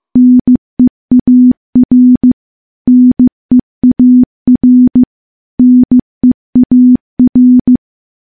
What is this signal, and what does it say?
Morse code, heard as a keyed tone, about 15 wpm, 250 Hz. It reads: NEAR NEAR NEAR
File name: morse / morse-training-15WPM